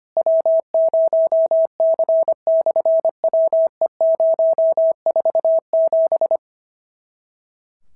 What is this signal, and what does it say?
Morse code, heard as a keyed tone, about 25 wpm, 650 Hz. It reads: W0C/WE047
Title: SOTA QSO Practice 25wpm